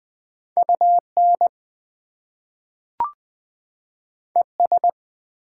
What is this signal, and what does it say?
Morse code, heard as a keyed tone, about 21 wpm, 700 Hz. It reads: UN ES
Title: Prefixes and Suffixes 20wpm